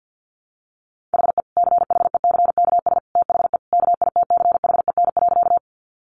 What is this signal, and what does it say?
Morse code, heard as a keyed tone, about 50 wpm, 700 Hz. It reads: HE CHECKS THE KITCHEN.
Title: Easy Phrases 50wpm